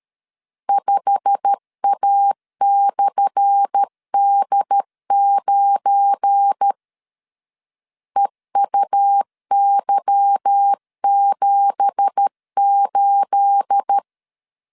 Morse code, heard as a keyed tone, about 13 wpm, 800 Hz. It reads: 5A/D9 EUY78
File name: LEZIONI / 10 SIGNS / 10A SLASH 12wpm 2314